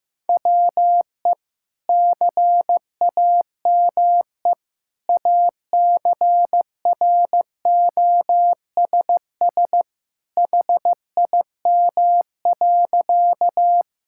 Morse code, heard as a keyed tone, about 15 wpm, 700 Hz. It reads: WE CAME ACROSS HIM.
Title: Sentences from Top 1000 Words - Review Part 5 - 15wpm